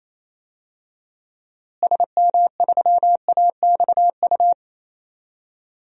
Morse code, heard as a keyed tone, about 28 wpm, 700 Hz. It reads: SM3AXU
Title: DXCC-100 Call Signs - 6 Characters - Repeated 3x - 28wpm